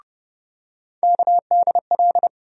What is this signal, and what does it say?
Morse code, heard as a keyed tone, about 31 wpm, 700 Hz. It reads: KDL
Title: USA Call Sign Postfixes - 3 Letters 30wpm